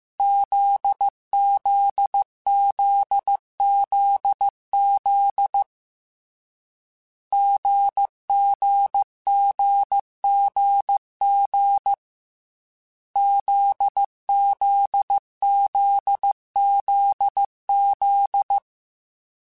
Morse code, heard as a keyed tone, about 15 wpm, 800 Hz. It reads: ZZZZZ GGGGG ZZZZZ
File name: Difficult Combos / Difficult Combos-mp3s / GZ GZ7 @13